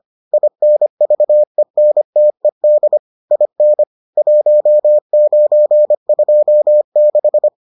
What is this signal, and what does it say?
Morse code, heard as a keyed tone, about 25 wpm, 600 Hz. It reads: INVENTED IN 1926